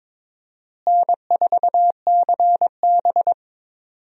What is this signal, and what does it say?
Morse code, heard as a keyed tone, about 22 wpm, 700 Hz. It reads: N4CB
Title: USA Call Signs - 4 Characters - Repeated 3x - 22wpm